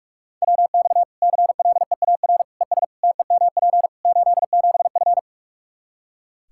Morse code, heard as a keyed tone, about 46 wpm, 700 Hz. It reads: WX CLEAR ES TEMP 87F
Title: Practice QSO - Exchange 2 - Info - 45wpm